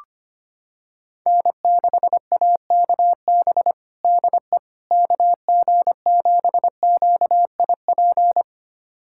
Morse code, heard as a keyed tone, about 25 wpm, 700 Hz. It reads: N6AKB DE KG7QIP
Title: USA Callsign DE Callsign 25wpm